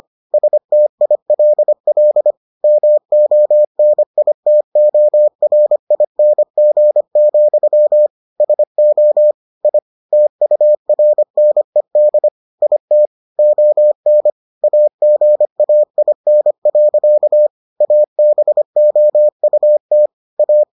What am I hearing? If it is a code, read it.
Morse code, heard as a keyed tone, about 25 wpm, 600 Hz. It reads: STILL MONITORING, SO I TURNED IT ON AGAIN. ABOUT A